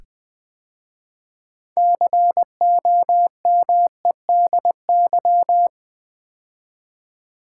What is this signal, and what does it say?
Morse code, heard as a keyed tone, about 20 wpm, 700 Hz. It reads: COMEDY
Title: 6 Letter Words - First 3 Letters Given - 20wpm